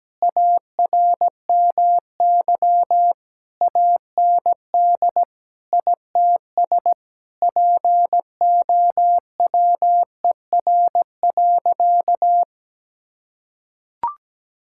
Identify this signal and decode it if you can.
Morse code, heard as a keyed tone, about 17 wpm, 700 Hz. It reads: ARMY AND ITS POWER.
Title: Sentences from Top 1000 Words - Review Part 5 - 17wpm